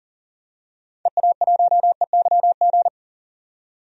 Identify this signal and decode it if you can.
Morse code, heard as a keyed tone, about 38 wpm, 700 Hz. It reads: EA1EYG
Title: DXCC-100 Call Signs - 6 Characters - 40wpm